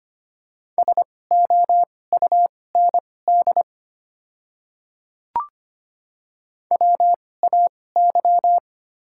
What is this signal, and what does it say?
Morse code, heard as a keyed tone, about 25 wpm, spaced as for 18 wpm, 690 Hz. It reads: SOUND WAY
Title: 2x Character Spacing - Top 100 Words - 25wpm